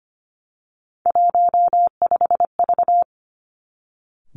Morse code, heard as a keyed tone, about 25 wpm, 700 Hz. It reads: 15V
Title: Three Challenging Letter-Number Combinations 25wpm